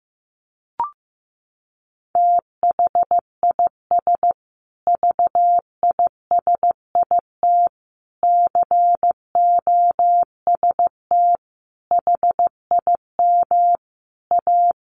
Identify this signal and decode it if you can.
Morse code, heard as a keyed tone, about 15 wpm, 700 Hz. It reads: THIS VISIT COST HIM A
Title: Sentences from Top 1400 Words 15wpm